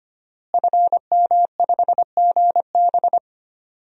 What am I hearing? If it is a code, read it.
Morse code, heard as a keyed tone, about 25 wpm, 700 Hz. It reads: FM5GB
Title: DXCC-100 Call Signs - 5 Characters - Repeated 3x - 25wpm